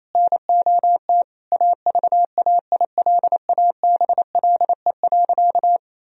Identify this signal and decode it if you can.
Morse code, heard as a keyed tone, about 28 wpm, 700 Hz. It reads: NOT AVAILABLE.
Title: Sentences from Top 1000 Words - Review Part 3 - 28wpm